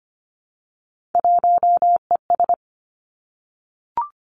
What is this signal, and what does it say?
Morse code, heard as a keyed tone, about 25 wpm, 700 Hz. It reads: 1ES
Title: Three Challenging Letter-Number Combinations 25wpm